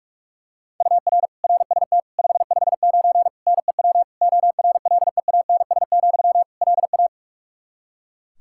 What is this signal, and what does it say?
Morse code, heard as a keyed tone, about 46 wpm, 700 Hz. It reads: UR RST 559 NEW ORLEANS, LA
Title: Practice QSO - Exchange 1 - Beginning - 45wpm